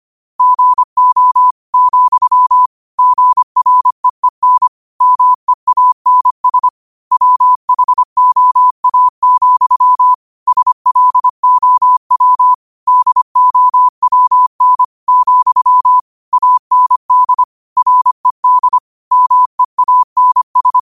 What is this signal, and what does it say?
Morse code, heard as a keyed tone, about 25 wpm, 1000 Hz. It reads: GO, GREEN MEANS WHOA, SLOW DOWN, AND RED MEANS